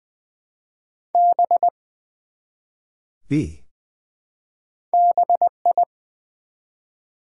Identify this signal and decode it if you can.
Morse code, heard as a keyed tone, about 20 wpm, 700 Hz. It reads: B BI
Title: Build A Word - 1 Letter At A Time - 3 to 5-Letter Words - 20wpm